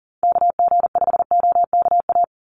Morse code, heard as a keyed tone, about 38 wpm, 700 Hz. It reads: KG5OKA
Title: USA Call Signs 40wpm